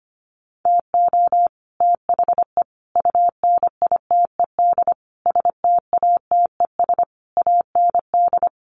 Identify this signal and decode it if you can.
Morse code, heard as a keyed tone, about 25 wpm, 700 Hz. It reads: TO THE UNITED STATES AND
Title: Sets of 5 Words Encore 25wpm